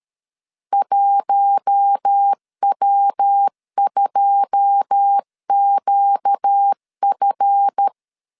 Morse code, heard as a keyed tone, about 13 wpm, 800 Hz. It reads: 1W2QF
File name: LEZIONI / 9 2 12wpm 2301